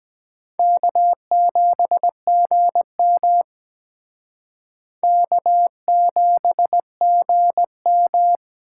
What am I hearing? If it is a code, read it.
Morse code, heard as a keyed tone, about 21 wpm, 700 Hz. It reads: K7GM K7GM
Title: Speed Racing - USA Call Signs 17wpm